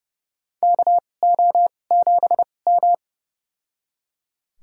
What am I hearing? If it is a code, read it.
Morse code, heard as a keyed tone, about 30 wpm, spaced as for 22 wpm, 700 Hz. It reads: KO7M
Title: USA Call Signs - 4 Characters - 2x Character Spacing - 30wpm